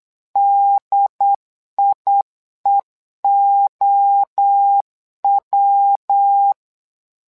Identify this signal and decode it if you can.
Morse code, heard as a keyed tone, about 8 wpm, 800 Hz. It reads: DIEOW
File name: LEZIONI / 3 MWRO / 3 MWRO 8wpm 2606